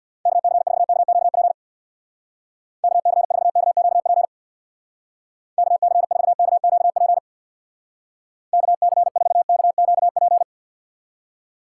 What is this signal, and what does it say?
Morse code, heard as a keyed tone, about 61 wpm, 700 Hz. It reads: KC4KQP KC4KQP KC4KQP KC4KQP
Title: Speed Racing - USA Call Signs 50wpm